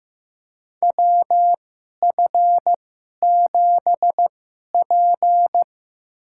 Morse code, heard as a keyed tone, about 15 wpm, spaced as for 11 wpm, 700 Hz. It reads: WF7P
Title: USA Call Signs - 2x Character Spacing - 15wpm